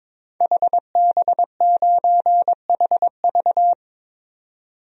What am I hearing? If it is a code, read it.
Morse code, heard as a keyed tone, about 21 wpm, 700 Hz. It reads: HB9HV
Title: DXCC-100 Call Signs - 5 Characters - 22wpm